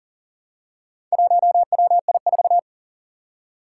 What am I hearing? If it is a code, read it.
Morse code, heard as a keyed tone, about 40 wpm, 700 Hz. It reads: 1WI4
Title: Four Challenging Letter-Number Combinations 40wpm